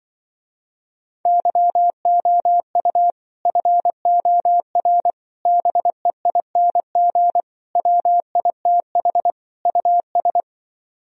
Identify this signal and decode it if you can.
Morse code, heard as a keyed tone, about 24 wpm, 700 Hz. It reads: YOU FOR BEING WITH US
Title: Speed-Racing - Sets of 5 Words 20wpm